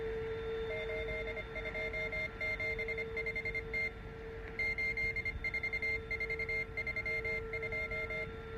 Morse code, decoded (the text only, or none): 8275T 84432